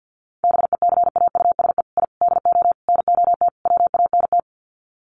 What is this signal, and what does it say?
Morse code, heard as a keyed tone, about 50 wpm, 700 Hz. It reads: BECAUSE I DO NOT WANT
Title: Sets of 5 Words Encore 50wpm